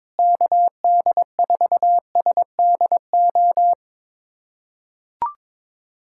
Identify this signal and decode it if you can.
Morse code, heard as a keyed tone, about 22 wpm, 700 Hz. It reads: KD4SDO
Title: USA Callsign DE Callsign 22wpm